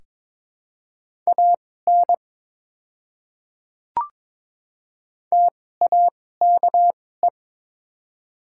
Morse code, heard as a keyed tone, about 22 wpm, spaced as for 16 wpm, 700 Hz. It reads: AN TAKE
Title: 2x Character Spacing - Top 200 Words - 22wpm